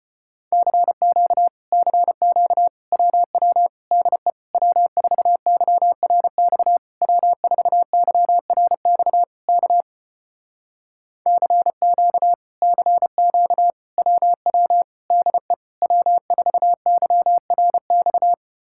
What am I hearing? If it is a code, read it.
Morse code, heard as a keyed tone, about 33 wpm, 700 Hz. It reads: CQ CQ WW DE W4YRX W4YRX K CQ CQ WW DE W4YRX